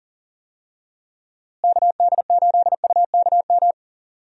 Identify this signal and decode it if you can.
Morse code, heard as a keyed tone, about 40 wpm, 700 Hz. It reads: KD8UKM
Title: USA Call Signs - 6 Characters - Repeated 2x - 40wpm